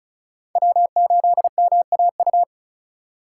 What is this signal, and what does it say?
Morse code, heard as a keyed tone, about 36 wpm, 700 Hz. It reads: W8MAU
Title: USA Call Signs - 3x Character Spacing - 35wpm